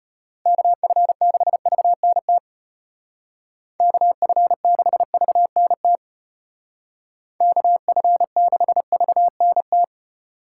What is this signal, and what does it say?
Morse code, heard as a keyed tone, about 30 wpm, 700 Hz. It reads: KF6VNT KF6VNT KF6VNT
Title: Speed Racing - USA 2x3 Call Signs 25wpm